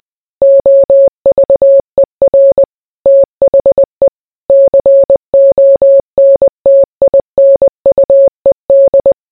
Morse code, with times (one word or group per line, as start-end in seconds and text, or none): 0.4-2.6 OVER
3.1-4.1 THE
4.5-9.1 CONTINUED